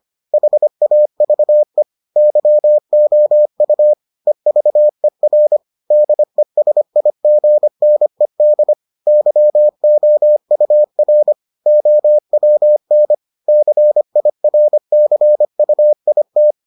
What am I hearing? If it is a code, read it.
Morse code, heard as a keyed tone, about 25 wpm, 600 Hz. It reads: HAVE YOU EVER DESIGNED YOUR OWN CIRCUIT